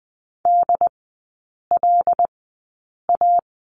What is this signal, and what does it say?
Morse code, heard as a keyed tone, about 20 wpm, spaced as for 8 wpm, 700 Hz. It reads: DLA
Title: 127 Lesson 21e - Number 6 - Three Characters 20wpm